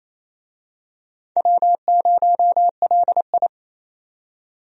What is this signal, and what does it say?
Morse code, heard as a keyed tone, about 28 wpm, 700 Hz. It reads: W0LI
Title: USA Call Signs - 4 Characters - 28wpm